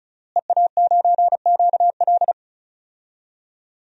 Morse code, heard as a keyed tone, about 35 wpm, 700 Hz. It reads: EA9QL
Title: DXCC-100 Call Signs - 5 Characters - Repeated 3x - 35wpm